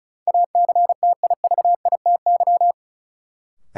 Morse code, heard as a keyed tone, about 36 wpm, 700 Hz. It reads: ACTIVITY
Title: Sentences from Top 1000 Words - Review Part 4 - New Word Drill - 35wpm